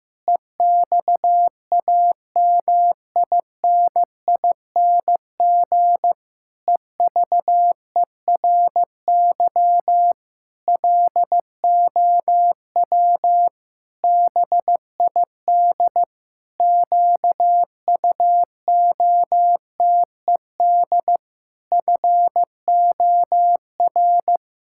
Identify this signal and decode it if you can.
Morse code, heard as a keyed tone, about 15 wpm, 700 Hz. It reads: EXAMINING EVERY LOW BID QUOTED FOR
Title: Pangram Challenge v2 - 15wpm